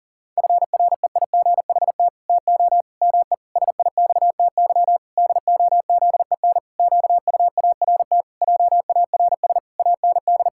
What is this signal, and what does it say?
Morse code, heard as a keyed tone, about 40 wpm, 700 Hz. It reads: FREIGHT TO ME SIXTY DOZEN QUART JARS AND
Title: Pangram Challenge v2 - 40wpm